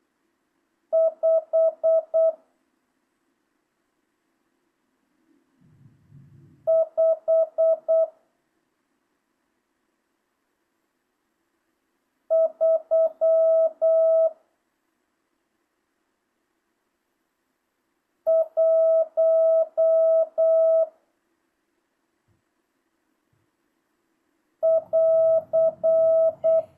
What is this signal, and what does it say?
Morse code, heard as a keyed tone, about 8 wpm, 650 Hz. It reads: 5 5 3 1 <AR>
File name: jh8cbh / 2025cw